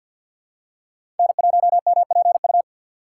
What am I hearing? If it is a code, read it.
Morse code, heard as a keyed tone, about 52 wpm, 690 Hz. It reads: N1MPU
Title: USA Call Signs - 5 Characters - 3x Character Spacing - 50wpm